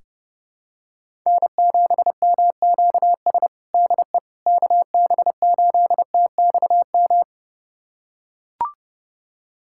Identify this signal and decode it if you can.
Morse code, heard as a keyed tone, about 30 wpm, 700 Hz. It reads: N7MQS DE KB8TXM
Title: USA Callsign DE Callsign 30wpm